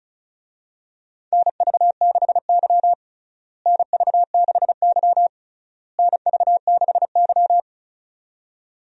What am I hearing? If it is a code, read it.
Morse code, heard as a keyed tone, about 34 wpm, 700 Hz. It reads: NV6Y NV6Y NV6Y
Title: USA Call Signs - 4 Characters - Repeated 3x - 35wpm